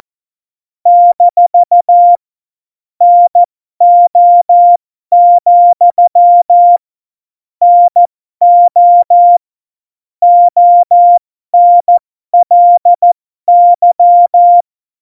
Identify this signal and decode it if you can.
Morse code, heard as a keyed tone, about 14 wpm, spaced as for 12 wpm, 700 Hz. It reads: - NO, NO ONLY
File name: Morsekurs / Witze